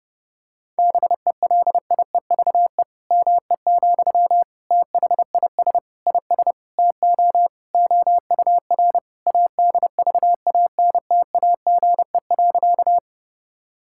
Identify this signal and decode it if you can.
Morse code, heard as a keyed tone, about 32 wpm, 700 Hz. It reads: BELIEVE ME, THIS IS TO OUR ADVANTAGE.
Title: Sentences from Top 1000 Words - Review Part 6 - 30wpm